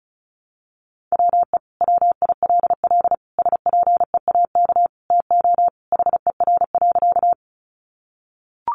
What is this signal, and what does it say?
Morse code, heard as a keyed tone, about 34 wpm, 700 Hz. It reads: WE WILL SPEAK TO HER.
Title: Sentences from Top 1100 Words 35wpm